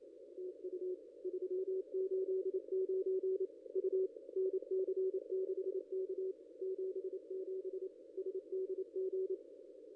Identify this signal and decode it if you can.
Morse code, heard as a keyed tone, about 28 wpm, 380 Hz. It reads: TU 389 U NC6K 77 SDG
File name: contests / ss-cw-k5tr-op-n5rz -0400